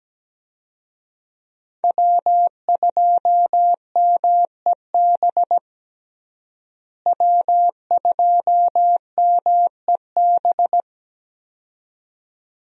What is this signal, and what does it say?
Morse code, heard as a keyed tone, about 17 wpm, 700 Hz. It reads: W2MEB W2MEB
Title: USA Call Signs - 5 Characters - Repeated 2x - 17wpm